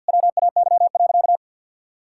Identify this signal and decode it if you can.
Morse code, heard as a keyed tone, about 50 wpm, 700 Hz. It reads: WAY.
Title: Sentences from Top 1000 Words - Review Part 6 - 50wpm